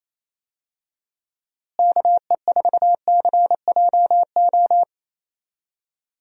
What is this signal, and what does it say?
Morse code, heard as a keyed tone, about 28 wpm, 700 Hz. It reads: KE4CJO
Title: USA Call Signs - 6 Characters - Repeated 2x - 28wpm